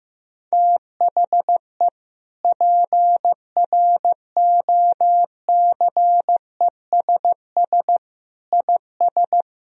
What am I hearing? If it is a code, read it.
Morse code, heard as a keyed tone, about 15 wpm, 700 Hz. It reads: THE PROCESS IS
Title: Sentences from Top 1000 Words - Review Part 3 - 15wpm